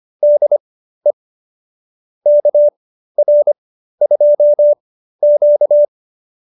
Morse code, heard as a keyed tone, about 25 wpm, spaced as for 13 wpm, 590 Hz. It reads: DE KR2Q